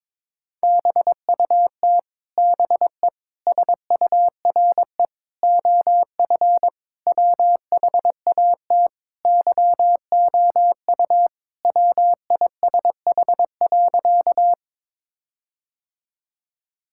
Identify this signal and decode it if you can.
Morse code, heard as a keyed tone, about 21 wpm, 700 Hz. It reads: BUT BE SURE OF WHAT YOU WISH.
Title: Sentences from Top 1600 Words 22wpm